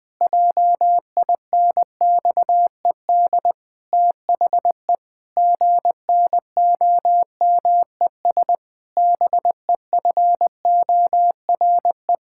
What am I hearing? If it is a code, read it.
Morse code, heard as a keyed tone, about 20 wpm, 700 Hz. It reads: JINXED THE GNOMES BEFORE
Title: Pangram Challenge v2 - 20wpm